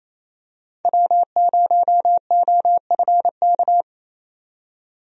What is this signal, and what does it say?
Morse code, heard as a keyed tone, about 28 wpm, 700 Hz. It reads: W0OFK
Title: USA Call Signs - 5 Characters - 28wpm